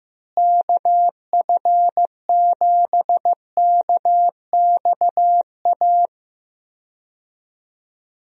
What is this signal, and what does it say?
Morse code, heard as a keyed tone, about 15 wpm, 700 Hz. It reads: KF7KXA
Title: USA Call Signs - 6 Characters - Repeated 3x - 15wpm